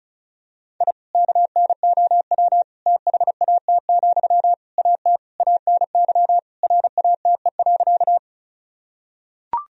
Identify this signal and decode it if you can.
Morse code, heard as a keyed tone, about 36 wpm, 700 Hz. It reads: I KNOW THAT, AT ANY RATE.
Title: Sentences from Top 1000 Words - Review Part 4 - 35wpm